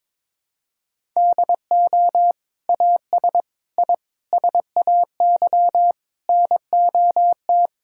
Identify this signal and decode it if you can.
Morse code, heard as a keyed tone, about 21 wpm, 700 Hz. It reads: DO AS I SAY NOT
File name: practice / 22wpm / Sentences with 8-Word Phrases - 22wpm